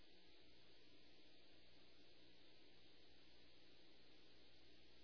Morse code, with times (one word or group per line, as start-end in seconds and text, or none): none